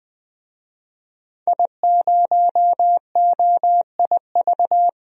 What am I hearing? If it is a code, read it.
Morse code, heard as a keyed tone, about 20 wpm, 700 Hz. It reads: I0OIV
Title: DXCC-100 Call Signs - 5 Characters - 20wpm